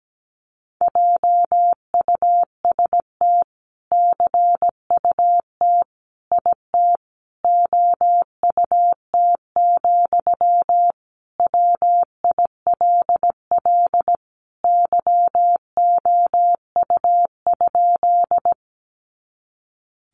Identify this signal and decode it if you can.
Morse code, heard as a keyed tone, about 17 wpm, 700 Hz. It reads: JUST CUT IT OUT, WILL YOU?
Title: Sentences from Top 300 Words 17wpm